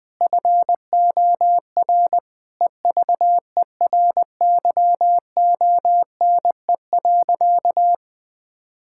Morse code, heard as a keyed tone, about 20 wpm, 700 Hz. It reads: FOR EVERYONE.
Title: Sentences from Top 1000 Words - Review Part 5 - 20wpm